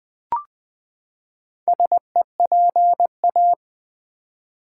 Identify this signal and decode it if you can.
Morse code, heard as a keyed tone, about 20 wpm, 700 Hz. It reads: SEPA
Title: Build A Word - 4 Letters At A Time - 8-Letter Words - 20wpm